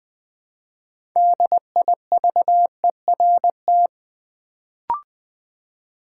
Morse code, heard as a keyed tone, about 20 wpm, 700 Hz. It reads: DIVERT
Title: Build A Word - 2 Letters At A Time - 6-Letter Words - 20wpm